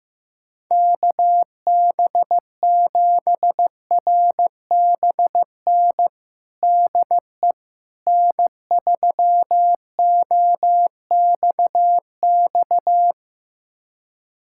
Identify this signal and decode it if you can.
Morse code, heard as a keyed tone, about 16 wpm, 700 Hz. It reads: KB7RBN DE N3OXX
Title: USA Callsign DE Callsign 15wpm